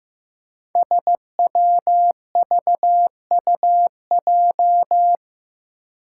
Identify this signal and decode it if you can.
Morse code, heard as a keyed tone, about 15 wpm, 700 Hz. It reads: SWVUJ
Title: Five Challenging Letter-Number Combinations 15wpm